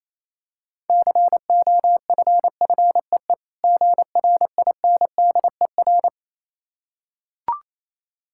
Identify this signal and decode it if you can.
Morse code, heard as a keyed tone, about 27 wpm, 700 Hz. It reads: COFFEE GRINDER
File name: Easy Words - Speed-Racing with Hints - 22wpm